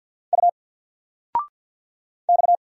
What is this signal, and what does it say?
Morse code, heard as a keyed tone, about 50 wpm, 700 Hz. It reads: U X
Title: Single Letter-Number 50wpm